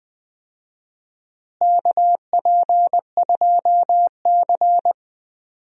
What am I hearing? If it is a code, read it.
Morse code, heard as a keyed tone, about 21 wpm, 700 Hz. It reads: KP2C
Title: DXCC-100 Call Signs - 4 Characters - Repeated 3x - 20wpm